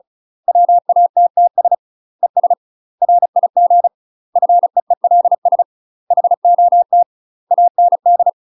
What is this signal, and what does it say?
Morse code, heard as a keyed tone, about 34 wpm, 700 Hz. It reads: WATTS ES RIG FEELS HOT AND
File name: UK MASTER QSOs TRAINING / 35WPM / UK MASTER QSOs 0019